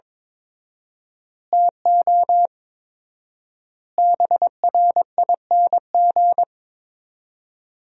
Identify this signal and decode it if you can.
Morse code, heard as a keyed tone, about 22 wpm, 700 Hz. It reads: TO BRING
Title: 4x Word-Spacing - Sentences from Top 500 Words - 22wpm